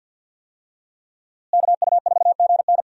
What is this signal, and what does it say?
Morse code, heard as a keyed tone, about 46 wpm, 700 Hz. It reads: KU4GN
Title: USA Call Signs - 5 Characters - Repeated 3x - 50wpm